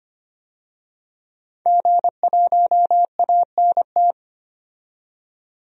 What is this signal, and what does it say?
Morse code, heard as a keyed tone, about 25 wpm, 700 Hz. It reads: G1ANT
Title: DXCC-100 Call Signs - 5 Characters - 25wpm